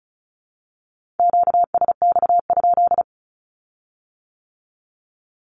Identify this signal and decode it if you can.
Morse code, heard as a keyed tone, about 36 wpm, 700 Hz. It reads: QSX?
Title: Q-Codes with Spoken Meaning - 35wpm